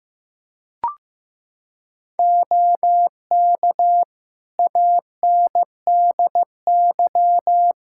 Morse code, heard as a keyed tone, about 15 wpm, 700 Hz. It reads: OK ANDY
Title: K1USN SST QSO Practice 15wpm